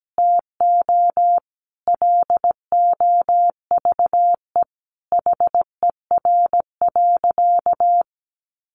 Morse code, heard as a keyed tone, about 17 wpm, 700 Hz. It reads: TO LOVE HER.
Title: Sentences from Top 400 Words 17wpm